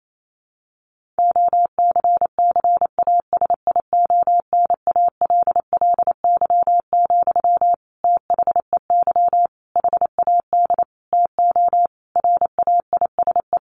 Morse code, heard as a keyed tone, about 28 wpm, 700 Hz. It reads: OCCASIONALLY, THEY HAD TO RAISE
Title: Sentences from Top 1400 Words 28wpm